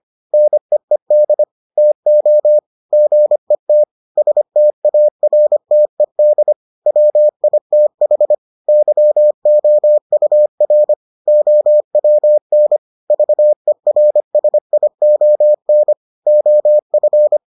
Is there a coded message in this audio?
Yes: Morse code from 0.3 to 17.4 s, heard as a keyed tone, about 25 wpm, 600 Hz. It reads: NEED TO GET STARTED WITH YOUR OWN VERSION OF